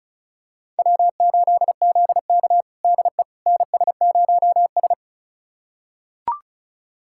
Morse code, heard as a keyed tone, about 36 wpm, 700 Hz. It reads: W8ZK DE NS0S
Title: USA Callsign DE Callsign - 4 Characters - 35wpm